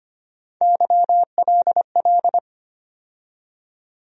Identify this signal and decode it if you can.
Morse code, heard as a keyed tone, about 25 wpm, 700 Hz. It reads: YLL
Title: USA Call Sign Postfixes - 3 Letters 25wpm